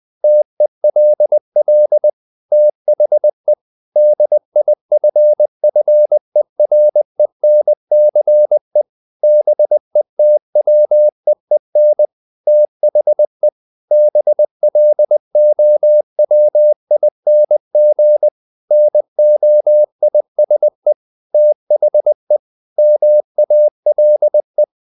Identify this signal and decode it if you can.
Morse code, heard as a keyed tone, about 20 wpm, 600 Hz. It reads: TELL THE DIFFERENCE BETWEEN THE BLOWING NOISE THE MALE
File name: the old man and the sea / the old man and the sea part 1 20wpm